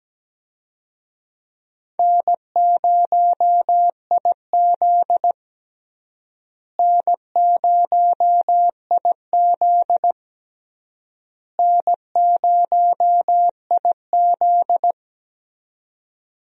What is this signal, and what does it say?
Morse code, heard as a keyed tone, about 17 wpm, 700 Hz. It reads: N0IZ N0IZ N0IZ
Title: USA Call Signs - 4 Characters - Repeated 3x - 17wpm